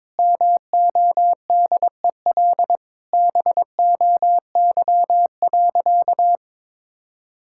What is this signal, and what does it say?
Morse code, heard as a keyed tone, about 22 wpm, 700 Hz. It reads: MODEL BOY.
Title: Sentences from Top 1000 Words - Review Part 5 - 22wpm